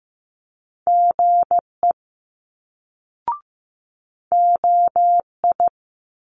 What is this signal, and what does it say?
Morse code, heard as a keyed tone, about 15 wpm, 700 Hz. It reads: GE OI
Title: Multi-letter Phonemes Quick Fire 15wpm